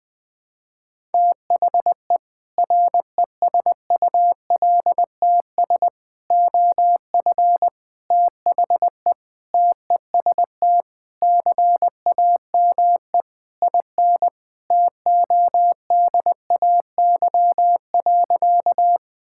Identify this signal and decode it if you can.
Morse code, heard as a keyed tone, about 20 wpm, 700 Hz. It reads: THE RESULTS OF THE TEST CAME IN TODAY.
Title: Sentences with 4-Word Phrases - Spoken Beginning - 20wpm